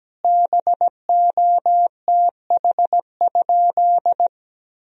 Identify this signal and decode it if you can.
Morse code, heard as a keyed tone, about 17 wpm, 700 Hz. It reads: BOTH?
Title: Sentences from Top 1000 Words - Review Part 1 - 17wpm